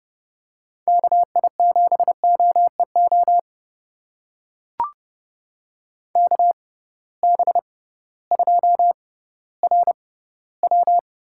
Morse code, heard as a keyed tone, about 30 wpm, spaced as for 16 wpm, 700 Hz. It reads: KI7OEO KB2RW
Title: USA Call Signs - 6 Characters - 6x Character Spacing - 30wpm